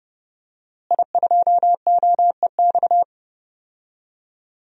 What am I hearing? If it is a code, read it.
Morse code, heard as a keyed tone, about 30 wpm, 700 Hz. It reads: I2OEX
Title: DXCC-100 Call Signs - 5 Characters - Repeated 2x - 30wpm